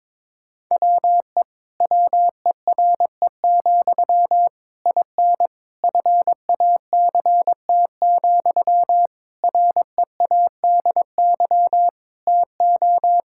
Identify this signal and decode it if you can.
Morse code, heard as a keyed tone, about 21 wpm, 700 Hz. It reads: WE WERE, IN FACT, READY TO
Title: Sentences from Top 1000 Words - Review Part 5 - 22wpm